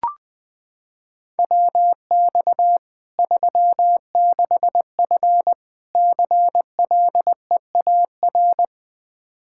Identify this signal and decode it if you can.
Morse code, heard as a keyed tone, about 20 wpm, 700 Hz. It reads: WX 36F CLEAR
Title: Practice QSO - Exchange 2 - Info - 20wpm